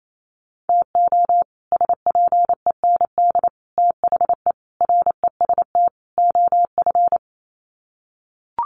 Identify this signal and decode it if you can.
Morse code, heard as a keyed tone, about 28 wpm, 700 Hz. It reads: TO SPEND THE REST OF
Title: Sets of 5 Words Encore2 28wpm